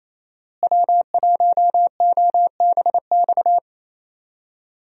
Morse code, heard as a keyed tone, about 28 wpm, 700 Hz. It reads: W1OBX
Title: USA Call Signs - 5 Characters - Repeated 3x - 28wpm